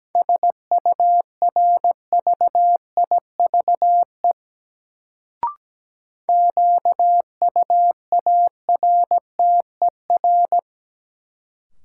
Morse code, heard as a keyed tone, about 17 wpm, 700 Hz. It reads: SURVIVE QUARTER
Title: Sentences from Top 1300 Words - New Word Drill Alt - 17wpm